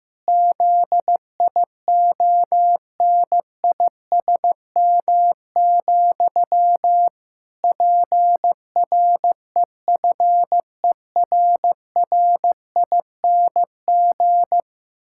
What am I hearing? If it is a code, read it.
Morse code, heard as a keyed tone, about 15 wpm, 700 Hz. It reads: ZIONISM, PREFERRING
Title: Pangram Challenge v2 - 15wpm